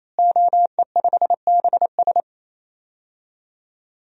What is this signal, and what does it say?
Morse code, heard as a keyed tone, about 28 wpm, 700 Hz. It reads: OE5BS
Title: DXCC-100 Call Signs - 5 Characters - Repeated 3x - 28wpm